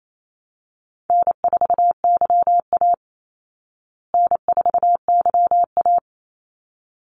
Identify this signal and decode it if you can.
Morse code, heard as a keyed tone, about 28 wpm, 700 Hz. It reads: N4YA N4YA
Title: Call Signs Repeated 3x - 28wpm